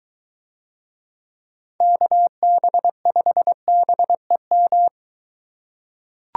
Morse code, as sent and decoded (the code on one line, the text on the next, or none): -.- -... ..... -... --
KB5BEM